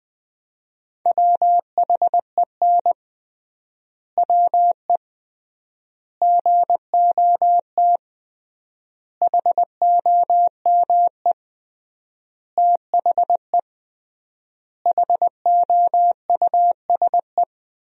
Morse code, heard as a keyed tone, about 20 wpm, 700 Hz. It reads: WHEN WE GOT HOME THE HOUSE
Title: 3x Word-Spacing - Sentences from Top 500 Words - 20wpm